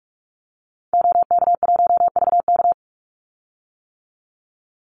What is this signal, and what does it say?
Morse code, heard as a keyed tone, about 44 wpm, 700 Hz. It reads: OK1VK
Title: Call Signs - 45wpm